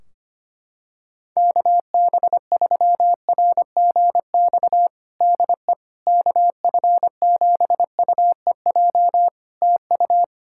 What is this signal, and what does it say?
Morse code, heard as a keyed tone, about 25 wpm, 700 Hz. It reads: KB3RGX DE KF7UEJ TU
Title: Practice QSO - Exchange 4 - Ending - 25wpm